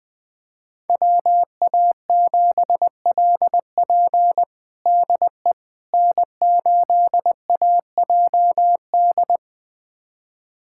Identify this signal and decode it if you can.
Morse code, heard as a keyed tone, about 21 wpm, 700 Hz. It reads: WA7LP DE N8AJD
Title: USA Callsign DE Callsign - 5 Characters - 20wpm